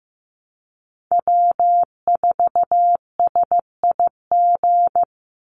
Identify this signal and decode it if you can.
Morse code, heard as a keyed tone, about 15 wpm, 700 Hz. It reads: W4SIG
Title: Call Signs Repeated 3x - 15wpm